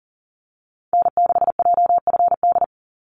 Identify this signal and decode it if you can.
Morse code, heard as a keyed tone, about 40 wpm, 700 Hz. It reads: N6JFD
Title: NA SOTA Activator Callsigns 40wpm